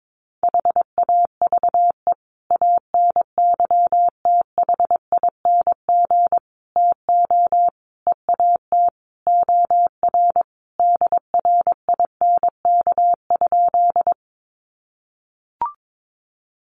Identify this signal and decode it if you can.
Morse code, heard as a keyed tone, about 22 wpm, 700 Hz. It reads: HAVE ANYTHING TO EAT OR DRINK?
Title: Sentences from Top 1500 Words 22wpm